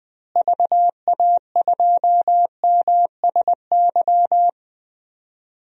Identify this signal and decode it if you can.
Morse code, heard as a keyed tone, about 20 wpm, 700 Hz. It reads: VA2MSY
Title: DXCC-100 Call Signs - 6 Characters - Repeated 3x - 20wpm